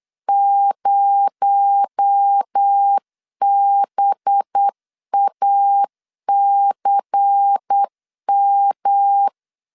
Morse code, heard as a keyed tone, about 8 wpm, 800 Hz. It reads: 0BACM
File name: 8wpm